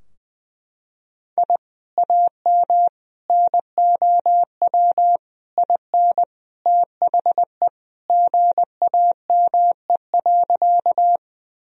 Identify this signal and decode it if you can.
Morse code, heard as a keyed tone, about 21 wpm, 700 Hz. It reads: I AM NOW IN THE GAME.
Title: Sentences from Top 1000 Words - Review Part 3 - 20wpm